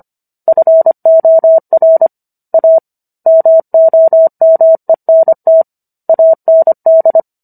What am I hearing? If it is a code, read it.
Morse code, heard as a keyed tone, about 25 wpm, 650 Hz. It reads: FOR A MOMENT AND